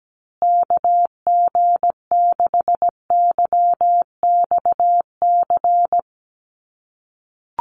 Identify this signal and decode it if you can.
Morse code, heard as a keyed tone, about 17 wpm, 700 Hz. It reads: KG6YXC
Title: USA Call Signs Repeated 2x 17wpm